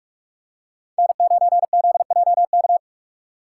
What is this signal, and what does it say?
Morse code, heard as a keyed tone, about 46 wpm, 700 Hz. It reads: N9ZJK
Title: USA Call Signs - 5 Characters - 3x Character Spacing - 45wpm